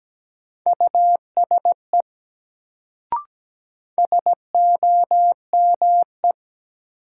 Morse code, heard as a keyed tone, about 17 wpm, 700 Hz. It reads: USE SOME
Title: Sentences from Top 100 Words - New Word Drill - 17wpm